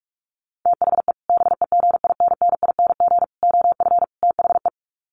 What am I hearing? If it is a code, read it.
Morse code, heard as a keyed tone, about 44 wpm, 700 Hz. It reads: THE BEGINNING OF THE
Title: Sets of 4 Words 45wpm